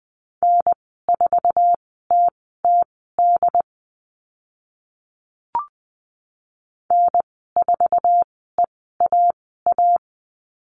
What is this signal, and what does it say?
Morse code, heard as a keyed tone, about 20 wpm, spaced as for 14 wpm, 700 Hz. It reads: N4TTD N4EAA
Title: 076 Lesson 13g - Letter L - Call Signs 20wpm